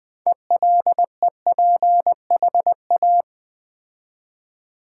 Morse code, heard as a keyed tone, about 20 wpm, 700 Hz. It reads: ELEPHA
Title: Build A Word - 3 Letters At A Time - 8-Letter Words - 20wpm